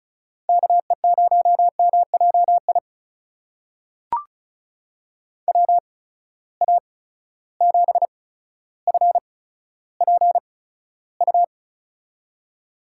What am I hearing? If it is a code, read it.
Morse code, heard as a keyed tone, about 36 wpm, spaced as for 15 wpm, 700 Hz. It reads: KE0MJI WA7FPU
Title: USA Call Signs - 6 Characters - 8x Character Spacing - 35wpm